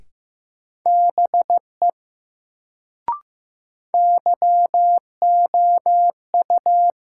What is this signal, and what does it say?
Morse code, heard as a keyed tone, about 15 wpm, 700 Hz. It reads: BE YOU
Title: Sentences from Top 100 Words - New Word Drill - 15wpm